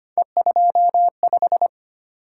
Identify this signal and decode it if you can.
Morse code, heard as a keyed tone, about 25 wpm, 700 Hz. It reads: E25
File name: Three Letter-Number Combinations - 25wpm